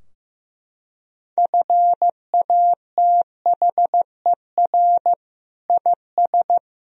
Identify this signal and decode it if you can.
Morse code, heard as a keyed tone, about 16 wpm, 700 Hz. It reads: FATHER IS
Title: Sentences from Top 1000 Words - Review Part 3 - 15wpm